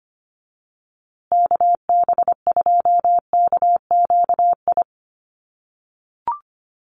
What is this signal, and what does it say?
Morse code, heard as a keyed tone, about 25 wpm, 700 Hz. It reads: KB2KQI
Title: USA Call Signs Repeated 3x 25wpm